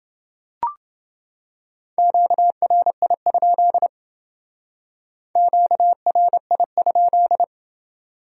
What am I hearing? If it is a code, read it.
Morse code, heard as a keyed tone, about 28 wpm, 700 Hz. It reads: QRI? QRI?